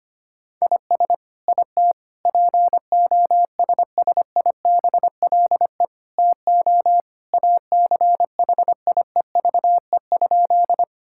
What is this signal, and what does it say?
Morse code, heard as a keyed tone, about 25 wpm, 700 Hz. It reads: IS IT POSSIBLE TO ACHIEVE?
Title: Sentences from Top 1000 Words - Review Part 6 - 25wpm